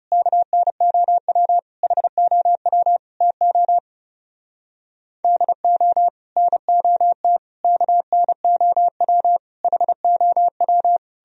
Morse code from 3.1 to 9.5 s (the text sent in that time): TO DO NOT KNOW